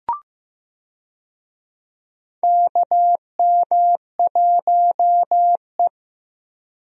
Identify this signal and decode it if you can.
Morse code, heard as a keyed tone, about 16 wpm, 700 Hz. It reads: KM1E
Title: USA Call Signs - 4 Characters - Repeated 2x - 15wpm